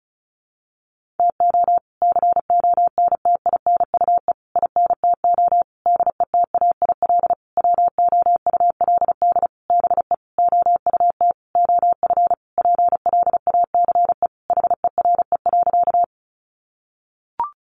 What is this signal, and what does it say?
Morse code, heard as a keyed tone, about 34 wpm, 700 Hz. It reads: TO CONTINUE INTO DETAIL WOULD BE OUT OF PLACE HERE.
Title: Sentences from Top 1500 Words 35wpm